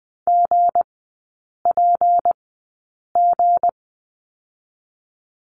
Morse code, heard as a keyed tone, about 20 wpm, spaced as for 8 wpm, 690 Hz. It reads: GPG
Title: 159 Lesson 26e - Letter g - Three Characters 20wpm